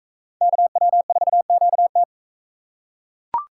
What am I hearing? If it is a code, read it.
Morse code, heard as a keyed tone, about 42 wpm, 700 Hz. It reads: KW4QT